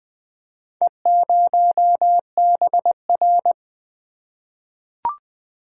Morse code, heard as a keyed tone, about 20 wpm, 700 Hz. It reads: E0BR
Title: Four Letter-Number Combinations - 20wpm